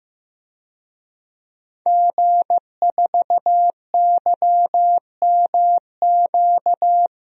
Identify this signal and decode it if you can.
Morse code, heard as a keyed tone, about 15 wpm, 700 Hz. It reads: G4YMQ
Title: DXCC-100 Call Signs - 5 Characters - 15wpm